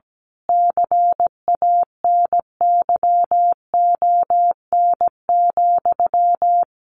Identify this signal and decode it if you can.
Morse code, heard as a keyed tone, about 17 wpm, 700 Hz. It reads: CANYON,
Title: A Sound Of Thunder 17wpm